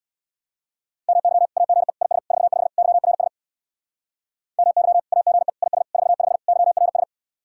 Morse code, heard as a keyed tone, about 64 wpm, 700 Hz. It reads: MY AGE IS 45 YRS MY AGE IS 45 YRS